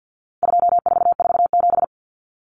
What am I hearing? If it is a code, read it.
Morse code, heard as a keyed tone, about 50 wpm, 700 Hz. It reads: 2447
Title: Numbers - 4-Digit 50wpm